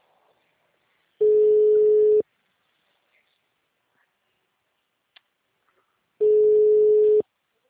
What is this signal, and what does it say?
Ringback tone (Europe, 425 Hz).